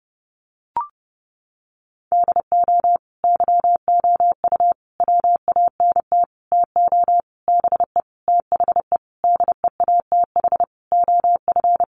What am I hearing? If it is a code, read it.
Morse code, heard as a keyed tone, about 30 wpm, 700 Hz. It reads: DO YOU WANT TO BE THE DEATH OF
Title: Sentences from Top 1100 Words 30wpm